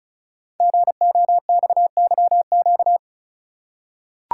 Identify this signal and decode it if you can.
Morse code, heard as a keyed tone, about 36 wpm, 700 Hz. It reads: GOXYQ
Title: Code Groups - 5 Letter:Numbers - 35wpm